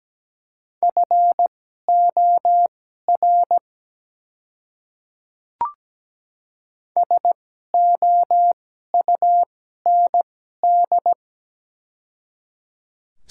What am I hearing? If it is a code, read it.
Morse code, heard as a keyed tone, about 17 wpm, spaced as for 12 wpm, 700 Hz. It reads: FOR SOUND
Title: 2x Character Spacing - Top 100 Words - 17wpm